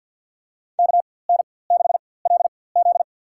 Morse code, heard as a keyed tone, about 48 wpm, spaced as for 23 wpm, 700 Hz. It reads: KN6LZ
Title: USA Call Signs - 4x Character Spacing - 50wpm